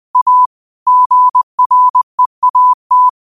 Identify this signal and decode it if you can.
Morse code, heard as a keyed tone, about 20 wpm, 1000 Hz. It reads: A GREAT
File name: qotdfiles / 20wpm